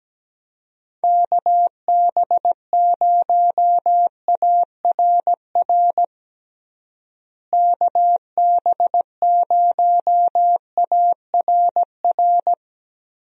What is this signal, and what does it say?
Morse code, heard as a keyed tone, about 17 wpm, 700 Hz. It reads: KB0ARR KB0ARR
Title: USA Call Signs - 6 Characters - Repeated 3x - 17wpm